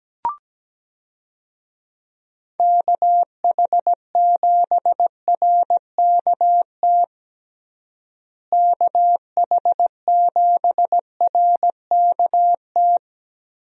Morse code, heard as a keyed tone, about 17 wpm, 700 Hz. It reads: KH7RKT KH7RKT
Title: USA Call Signs - 6 Characters - Repeated 3x - 17wpm